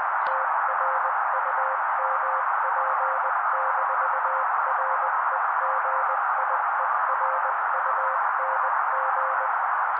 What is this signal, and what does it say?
Morse code, heard as a keyed tone, about 20 wpm, 530 Hz. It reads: TRUMP-REGIERUNG
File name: daily / 18wpm noise